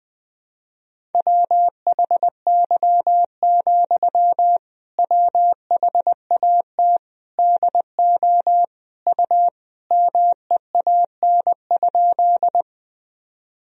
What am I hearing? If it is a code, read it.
Morse code, heard as a keyed tone, about 20 wpm, 700 Hz. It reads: WHY, WHAT DO U MEAN?
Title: Sentences from Top 200 Words and Abbreviations 20wpm